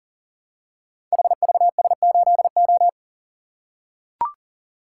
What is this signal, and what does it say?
Morse code, heard as a keyed tone, about 41 wpm, 700 Hz. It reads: HVS8O
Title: Code Groups - 5 Letter:Numbers - 40wpm